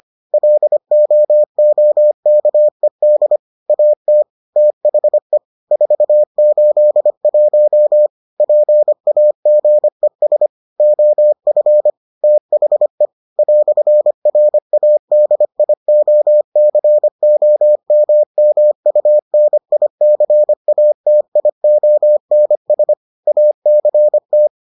Morse code, heard as a keyed tone, about 25 wpm, 600 Hz. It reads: LOOKED AT THE 481 PAGES OF THE "RADIOCOMMUNICATIONS ACT